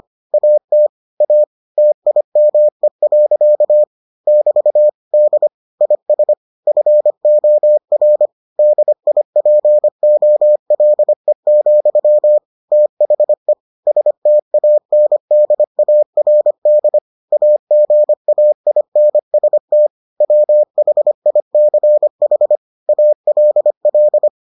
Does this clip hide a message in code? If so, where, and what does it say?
Yes: Morse code from 0.3 to 24.3 s, heard as a keyed tone, about 25 wpm, 600 Hz. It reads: AT A TIME. <BT> D IS FOR DIPOLE, THE STANDARD AGAINST WHICH ALL